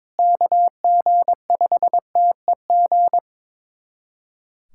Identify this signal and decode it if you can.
Morse code, heard as a keyed tone, about 22 wpm, 700 Hz. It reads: KG5TEG
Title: USA Callsign DE Callsign 22wpm